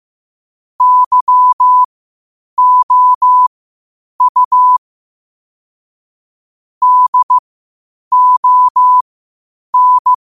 Morse code, heard as a keyed tone, about 15 wpm, spaced as for 8 wpm, 1000 Hz. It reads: YOU DON
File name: qotdfiles / 8wpm